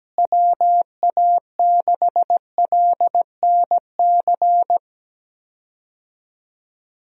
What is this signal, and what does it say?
Morse code, heard as a keyed tone, about 17 wpm, 700 Hz. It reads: WA6LNC
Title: USA Call Signs - 6 Characters - Repeated 3x - 17wpm